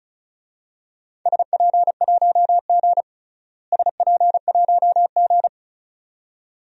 Morse code, heard as a keyed tone, about 36 wpm, 700 Hz. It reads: SP1G SP1G
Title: DXCC-100 Call Signs - 4 Characters - Repeated 2x - 35wpm